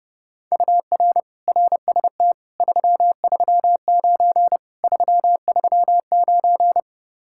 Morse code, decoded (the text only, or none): UR RST 339 339